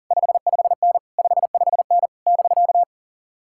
Morse code, heard as a keyed tone, about 50 wpm, 700 Hz. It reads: 55N 55N <BK>